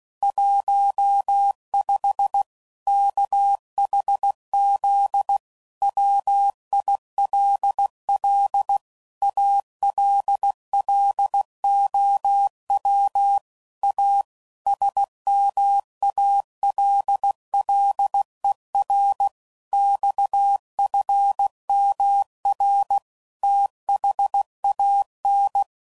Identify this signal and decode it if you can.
Morse code, heard as a keyed tone, about 16 wpm, 800 Hz. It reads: 15 KHZ WILL ALLOW A SMALLER XFMR THAN
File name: CW Code Groups @15WPM/Technical Subjects